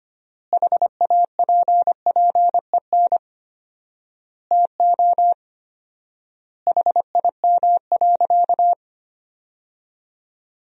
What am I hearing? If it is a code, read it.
Morse code, heard as a keyed tone, about 25 wpm, 700 Hz. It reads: HAPPEN TO HIM.
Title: 4x Word-Spacing - Sentences from Top 400 Words - 25wpm